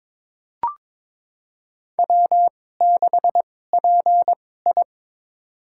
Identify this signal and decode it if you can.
Morse code, heard as a keyed tone, about 22 wpm, spaced as for 16 wpm, 700 Hz. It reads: W6PI
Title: USA Call Signs - 4 Characters - 2x Character Spacing - 22wpm